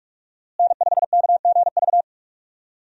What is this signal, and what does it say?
Morse code, heard as a keyed tone, about 46 wpm, 700 Hz. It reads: N5KGV
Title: USA Call Signs - 5 Characters - Repeated 2x - 45wpm